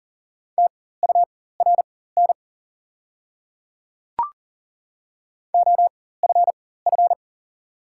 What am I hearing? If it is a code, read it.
Morse code, heard as a keyed tone, about 40 wpm, spaced as for 19 wpm, 700 Hz. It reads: TURN OFF
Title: 4x Character Spacing - Top 300 Words - 40wpm